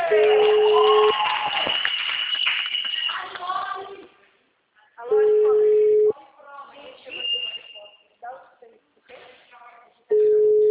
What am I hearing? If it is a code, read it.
Ringback tone (Europe, 425 Hz), with speech.